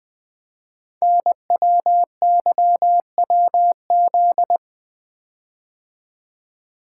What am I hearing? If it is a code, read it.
Morse code, heard as a keyed tone, about 20 wpm, 700 Hz. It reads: NWYWZ
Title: Code Groups - 5 Letters - 20wpm